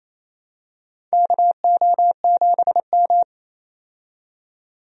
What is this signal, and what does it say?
Morse code, heard as a keyed tone, about 27 wpm, 700 Hz. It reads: KO7M
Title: USA Call Signs - 4 Characters - 28wpm